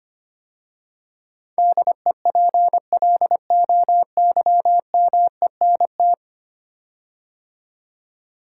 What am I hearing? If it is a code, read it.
Morse code, heard as a keyed tone, about 25 wpm, 700 Hz. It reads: DEPLOYMENT
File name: 10 Letter Words - First 3 Letters Given - 25wpm